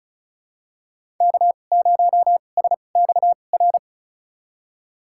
Morse code, heard as a keyed tone, about 35 wpm, spaced as for 25 wpm, 700 Hz. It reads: K0SXR
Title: USA Call Signs - 5 Characters - 2x Character Spacing - 35wpm